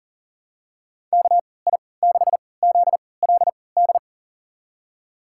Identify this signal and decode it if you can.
Morse code, heard as a keyed tone, about 40 wpm, spaced as for 23 wpm, 700 Hz. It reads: KI6ZLD
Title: USA Call Signs - 3x Character Spacing - 40wpm